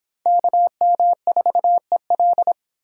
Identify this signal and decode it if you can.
Morse code, heard as a keyed tone, about 26 wpm, 700 Hz. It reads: KM4EL